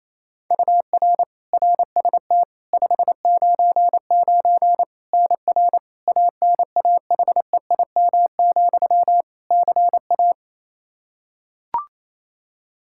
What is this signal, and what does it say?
Morse code, heard as a keyed tone, about 28 wpm, 700 Hz. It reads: UR RST 599 NR ANAHEIM, CA
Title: Practice QSO - Exchange 1 - Beginning - 28wpm